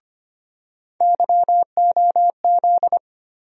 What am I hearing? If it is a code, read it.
Morse code, heard as a keyed tone, about 25 wpm, 700 Hz. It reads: YOZ
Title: USA Call Sign Postfixes - 3 Letters 25wpm